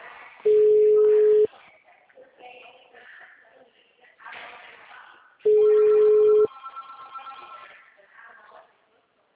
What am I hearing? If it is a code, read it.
Ringback tone (Europe, 425 Hz), with speech.